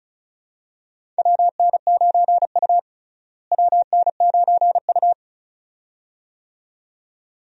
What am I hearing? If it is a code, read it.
Morse code, heard as a keyed tone, about 34 wpm, 700 Hz. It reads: WN9U WN9U
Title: USA Call Signs - 4 Characters - Repeated 2x - 35wpm